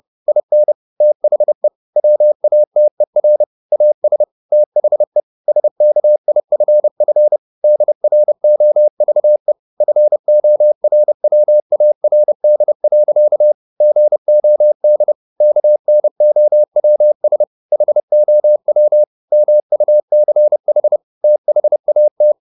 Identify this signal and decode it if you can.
Morse code, heard as a keyed tone, about 30 wpm, 600 Hz. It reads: IN THE WATER AS THE SKIFF DROVE FORWARD. GOD KNOWS HOW MUCH THAT